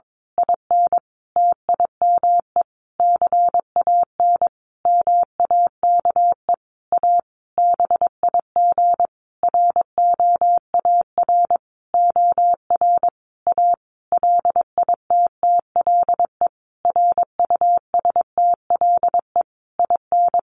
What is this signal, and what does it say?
Morse code, heard as a keyed tone, about 22 wpm, 700 Hz. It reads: IN TIME CAN MAKE A BIG ROAR OR A LITTLE RUSTLE IN
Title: A Sound Of Thunder 22wpm